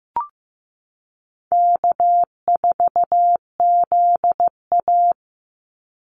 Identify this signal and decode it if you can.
Morse code, heard as a keyed tone, about 15 wpm, 700 Hz. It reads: K4ZA
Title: Easy Call Signs - 15wpm